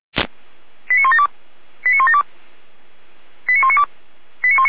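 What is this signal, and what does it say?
Five-tone selective call (ZVEI), heard as quick runs of five tones, tones of about 70 ms. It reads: 89172 89172 89192
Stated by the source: ZVEI five-tone selective call 89319 received on an SDR, archived by the Feuerwehr Orlamünde fire brigade.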